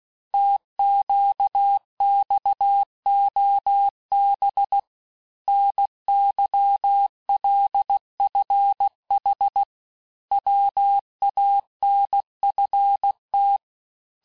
Morse code, heard as a keyed tone, about 16 wpm, 800 Hz. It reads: TQXOB NYLFH WANFT
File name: cours-du-lundi files / Cours-1586-texte-chiffre-vitesse-15mpm